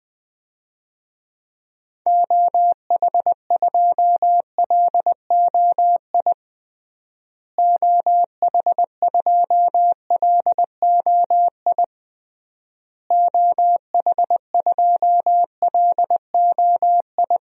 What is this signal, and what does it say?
Morse code, heard as a keyed tone, about 21 wpm, 700 Hz. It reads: OH2LOI OH2LOI OH2LOI
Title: DXCC-100 Call Signs - 6 Characters - Repeated 3x - 20wpm